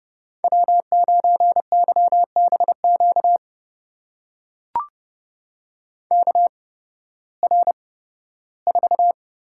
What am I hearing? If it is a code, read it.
Morse code, heard as a keyed tone, about 30 wpm, spaced as for 16 wpm, 700 Hz. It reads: W9YBQ KR4
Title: USA Call Signs - 5 Characters - 8x Character Spacing - 30wpm